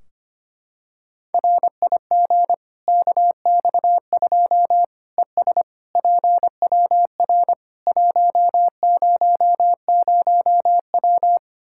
Morse code, heard as a keyed tone, about 25 wpm, 700 Hz. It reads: RIG KX2 ES PWR 100W
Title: Practice QSO - Exchange 2 - Info - 25wpm